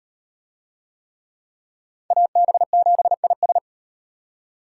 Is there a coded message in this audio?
Yes: Morse code from 2.1 to 3.6 s, heard as a keyed tone, about 38 wpm, 700 Hz. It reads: AB7IS